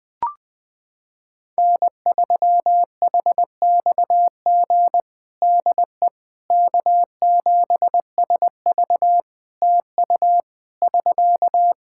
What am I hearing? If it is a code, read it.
Morse code, heard as a keyed tone, about 20 wpm, 700 Hz. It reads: N3HXG DE K7SV TU <SK>
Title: K1USN SST QSO Practice 20wpm